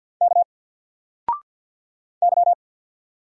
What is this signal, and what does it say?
Morse code, heard as a keyed tone, about 50 wpm, 700 Hz. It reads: K Y
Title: Single Letter-Number Alt 50wpm